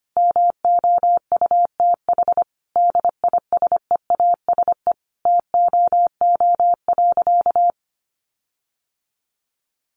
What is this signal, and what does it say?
Morse code, heard as a keyed tone, about 25 wpm, 700 Hz. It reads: MOUTH DISEASE TOO.
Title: Sentences from Top 1500 Words 25wpm